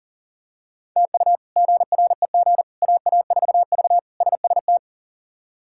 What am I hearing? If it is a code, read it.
Morse code, heard as a keyed tone, about 37 wpm, 700 Hz. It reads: TU GREG AA4V SST
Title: K1USN SST QSO Practice - 40wpm